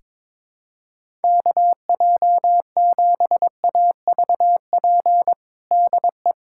Morse code, heard as a keyed tone, about 22 wpm, 700 Hz. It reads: KJ7AVP DE
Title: USA Callsign DE Callsign 22wpm